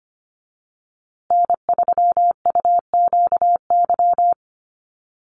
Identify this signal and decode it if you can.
Morse code, heard as a keyed tone, about 25 wpm, 700 Hz. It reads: N3UQY
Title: USA Call Signs Repeated 3x 25wpm